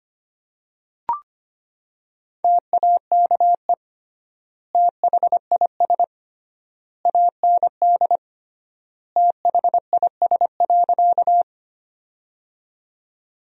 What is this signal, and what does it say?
Morse code, heard as a keyed tone, about 25 wpm, 700 Hz. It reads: TAKE THIS AND THIS.
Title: 3x Word-Spacing - Sentences from Top 200 Words - 25wpm